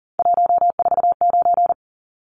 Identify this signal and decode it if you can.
Morse code, heard as a keyed tone, about 40 wpm, 700 Hz. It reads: 149
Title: Numbers - 3-Digit 40wpm